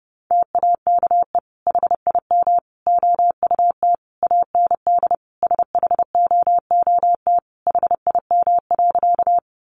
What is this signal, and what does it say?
Morse code, heard as a keyed tone, about 30 wpm, 700 Hz. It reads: TAKE HIM OUT AND SHOOT HIM.
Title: Sentences from Top 1200 Words 30wpm